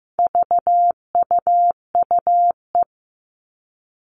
Morse code, heard as a keyed tone, about 15 wpm, 700 Hz. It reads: VUUE
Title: All Cut Numbers - 4-Digit 15wpm